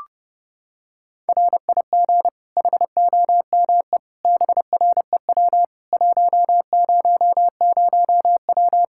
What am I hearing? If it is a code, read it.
Morse code, heard as a keyed tone, about 32 wpm, 700 Hz. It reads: RIG HOME BREW 100W
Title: Practice QSO - Exchange 2 - Info - 30wpm